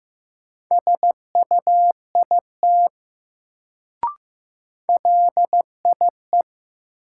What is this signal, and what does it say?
Morse code, heard as a keyed tone, about 15 wpm, 700 Hz. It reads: SUIT LIE
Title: Sentences from Top 1600 Words - New Word Drill - 15wpm